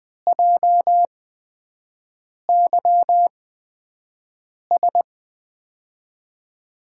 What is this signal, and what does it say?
Morse code, heard as a keyed tone, about 20 wpm, spaced as for 5 wpm, 700 Hz. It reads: JYS
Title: USA Call Signs - 6 Characters - 8x Character Spacing - 20wpm